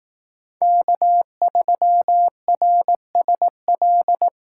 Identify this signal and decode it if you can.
Morse code, heard as a keyed tone, about 18 wpm, 700 Hz. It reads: K3RSL
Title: Speed Racing - USA 1x3 Call Signs 15wpm